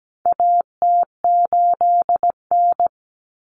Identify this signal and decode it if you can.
Morse code, heard as a keyed tone, about 17 wpm, 700 Hz. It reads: AT8N
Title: Common Cut Numbers - 4-Digit 17wpm